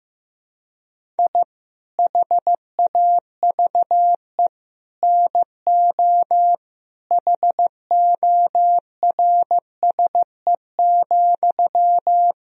Morse code, heard as a keyed tone, about 15 wpm, 700 Hz. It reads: I HAVE NO HORSE,
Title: Sentences from Top 1000 Words - Review Part 1 - 15wpm